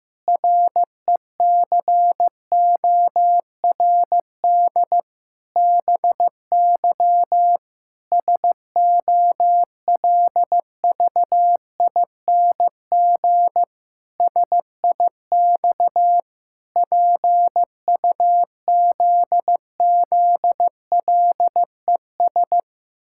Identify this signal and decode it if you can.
Morse code, heard as a keyed tone, about 15 wpm, 700 Hz. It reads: RECORD BY SOLVING SIX PUZZLES
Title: Pangram Challenge v2 - 15wpm